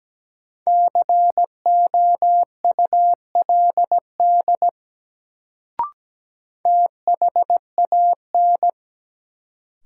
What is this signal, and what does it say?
Morse code, heard as a keyed tone, about 17 wpm, 700 Hz. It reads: COULD THAN
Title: Sets of 2 Words - New Word Drill - 17wpm